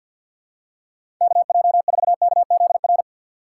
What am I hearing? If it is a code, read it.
Morse code, heard as a keyed tone, about 52 wpm, 700 Hz. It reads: KJ4KZR
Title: USA Call Signs - 6 Characters - 4x Character Spacing - 50wpm